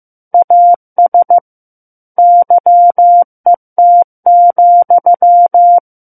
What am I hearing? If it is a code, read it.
Morse code, heard as a keyed tone, about 15 wpm, 700 Hz. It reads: AS YET,
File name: Book One / 06-Chapter 6 - 15 WPM